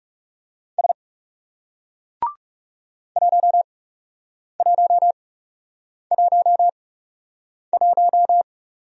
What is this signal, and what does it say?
Morse code, heard as a keyed tone, about 38 wpm, spaced as for 8 wpm, 700 Hz. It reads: S 1111